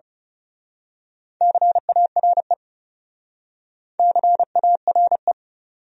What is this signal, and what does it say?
Morse code, heard as a keyed tone, about 34 wpm, 700 Hz. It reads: CARE CARE
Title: Speed-Racing Top 500 Words 30wpm